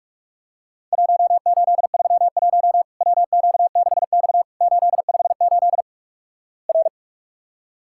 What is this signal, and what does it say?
Morse code, heard as a keyed tone, about 44 wpm, 700 Hz. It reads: 1831 WQ6X 858 R
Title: Worked All Europe DX Contest - QTC Exchange - 45wpm